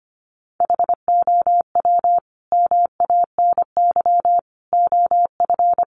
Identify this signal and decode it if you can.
Morse code, heard as a keyed tone, about 25 wpm, 700 Hz. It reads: HOW MANY OF
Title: Sentences from Top 1200 Words 25wpm